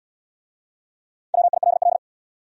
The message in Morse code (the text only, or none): WILL